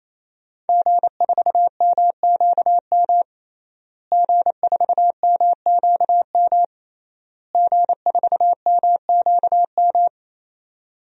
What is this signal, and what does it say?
Morse code, heard as a keyed tone, about 28 wpm, 700 Hz. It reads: G4MQM G4MQM G4MQM
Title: DXCC-100 Call Signs - 5 Characters - Repeated 3x - 28wpm